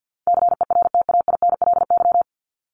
Morse code, heard as a keyed tone, about 50 wpm, 700 Hz. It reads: CERTAINLY
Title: LY Suffix - Top 100 Words with Base Words 50wpm